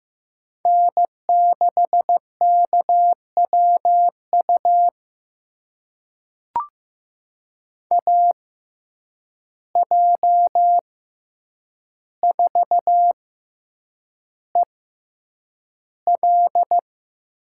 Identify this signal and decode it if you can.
Morse code, heard as a keyed tone, about 15 wpm, spaced as for 8 wpm, 700 Hz. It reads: N6KWU AJ4EL
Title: USA Call Signs - 5 Characters - 6x Character Spacing - 15wpm